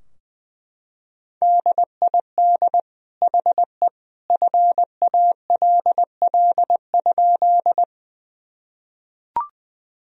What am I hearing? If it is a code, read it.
Morse code, heard as a keyed tone, about 20 wpm, 700 Hz. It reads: DID HE FALL?
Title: Sentences from Top 1000 Words - Review Part 2 - 20wpm